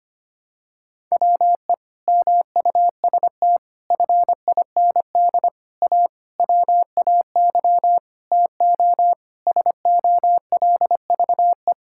Morse code, heard as a keyed tone, about 25 wpm, 700 Hz. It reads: WE MUST FIND A WAY TO SOLVE
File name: practice / 25wpm / Sentences with 4-Word Phrases - 25wpm